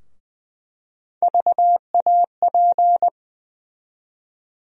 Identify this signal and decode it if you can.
Morse code, heard as a keyed tone, about 20 wpm, 700 Hz. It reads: VAP
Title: USA Call Sign Postfixes - 3 Letters 20wpm